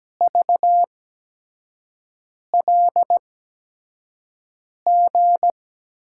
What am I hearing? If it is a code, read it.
Morse code, heard as a keyed tone, about 17 wpm, spaced as for 5 wpm, 700 Hz. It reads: VLG
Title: USA Call Signs - 5 Characters - 8x Character Spacing - 17wpm